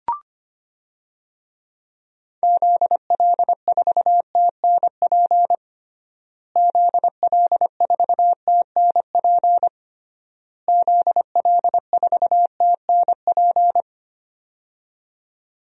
Morse code, heard as a keyed tone, about 25 wpm, 700 Hz. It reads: ZL4TNP ZL4TNP ZL4TNP
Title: DXCC-100 Call Signs - 6 Characters - Repeated 3x - 25wpm